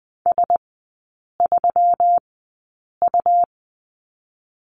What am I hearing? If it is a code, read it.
Morse code, heard as a keyed tone, about 19 wpm, spaced as for 8 wpm, 700 Hz. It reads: S3U
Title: 138 Lesson 23e - Letter f - Three Characters 20wpm